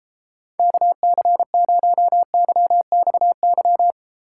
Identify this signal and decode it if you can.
Morse code, heard as a keyed tone, about 33 wpm, 700 Hz. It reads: KC0YXY